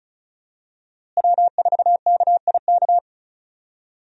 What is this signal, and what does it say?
Morse code, heard as a keyed tone, about 36 wpm, 700 Hz. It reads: W4KIK
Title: USA Call Signs - 5 Characters - Repeated 2x - 35wpm